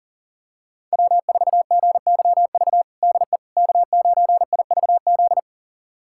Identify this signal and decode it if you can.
Morse code, heard as a keyed tone, about 40 wpm, 700 Hz. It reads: W4GYV DE K9IVZ
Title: USA Callsign DE Callsign - 5 Characters - 40wpm